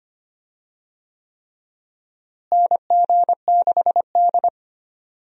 Morse code, heard as a keyed tone, about 25 wpm, 700 Hz. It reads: NG6D
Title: USA Call Signs - 4 Characters - Repeated 3x - 25wpm